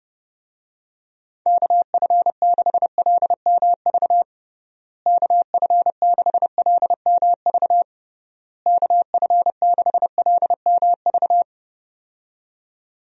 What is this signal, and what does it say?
Morse code, heard as a keyed tone, about 30 wpm, 700 Hz. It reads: KF6LMV KF6LMV KF6LMV
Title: USA Call Signs - 6 Characters - Repeated 3x - 30wpm